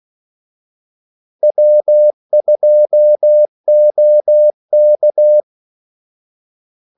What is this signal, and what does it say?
Morse code, heard as a keyed tone, about 16 wpm, 600 Hz. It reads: W2OK
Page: Kaldesignaler 16 wpm